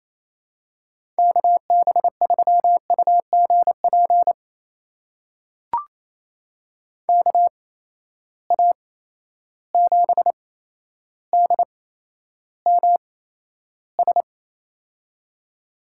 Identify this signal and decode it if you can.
Morse code, heard as a keyed tone, about 28 wpm, spaced as for 12 wpm, 700 Hz. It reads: KB3UGP KA7DMS
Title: USA Call Signs - 8x Character Spacing - 28wpm